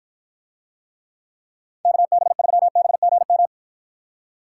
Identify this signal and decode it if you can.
Morse code, heard as a keyed tone, about 54 wpm, 700 Hz. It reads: KB3BGM